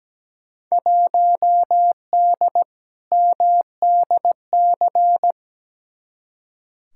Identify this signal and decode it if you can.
Morse code, heard as a keyed tone, about 17 wpm, 700 Hz. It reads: 1D MDC
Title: Field Day Exchanges - 17wpm